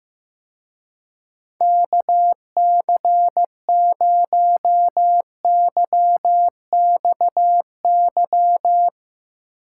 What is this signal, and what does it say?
Morse code, heard as a keyed tone, about 15 wpm, 700 Hz. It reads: KC0YXY
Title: Speed Racing - USA Call Signs 15wpm